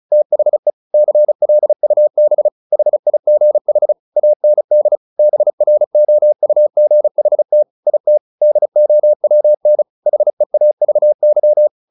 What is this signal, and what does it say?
Morse code, heard as a keyed tone, about 34 wpm, 600 Hz. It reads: THE CLUB HIGH AND BROUGHT IT DOWN HEAVY
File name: the old man and the sea 2 35wpm